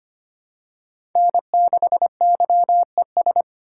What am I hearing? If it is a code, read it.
Morse code, heard as a keyed tone, about 25 wpm, 700 Hz. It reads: N6YES
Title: USA Call Signs - 5 Characters - 2x Character Spacing - 25wpm